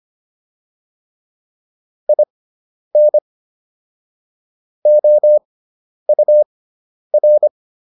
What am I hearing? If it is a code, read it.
Morse code, heard as a keyed tone, about 25 wpm, spaced as for 10 wpm, 600 Hz. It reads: IN OUR